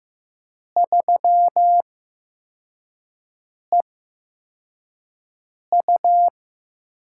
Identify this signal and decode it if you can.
Morse code, heard as a keyed tone, about 15 wpm, spaced as for 4 wpm, 700 Hz. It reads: 3EU
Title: USA Call Signs - 5 Characters - 8x Character Spacing - 15wpm